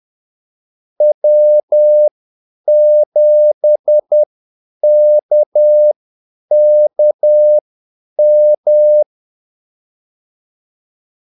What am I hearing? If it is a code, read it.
Morse code, heard as a keyed tone, about 10 wpm, spaced as for 8 wpm, 600 Hz. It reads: W7KKM
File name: Callsigns-10-8wpm